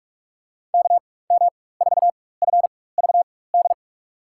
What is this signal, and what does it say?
Morse code, heard as a keyed tone, about 46 wpm, spaced as for 21 wpm, 700 Hz. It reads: KM4FVD
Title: USA Call Signs - 6 Characters - 4x Character Spacing - 45wpm